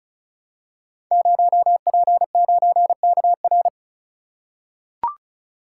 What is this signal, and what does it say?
Morse code, heard as a keyed tone, about 36 wpm, 700 Hz. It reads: 0P9KR
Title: Code Groups - 5 Letter:Numbers - 35wpm